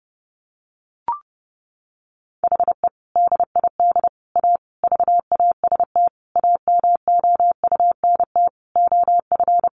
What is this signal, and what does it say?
Morse code, heard as a keyed tone, about 30 wpm, 700 Hz. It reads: HE DID A VAST AMOUNT OF
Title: Sentences from Top 1300 Words 30wpm